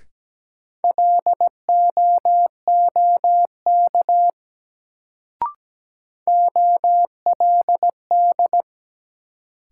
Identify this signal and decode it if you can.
Morse code, heard as a keyed tone, about 17 wpm, 700 Hz. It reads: LOOK OLD
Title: Sets of 2 Words - New Word Drill - 17wpm